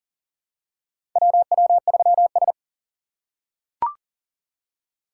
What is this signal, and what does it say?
Morse code, heard as a keyed tone, about 40 wpm, 700 Hz. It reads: WW3S
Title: USA Call Signs - 6x Character Spacing - 40wpm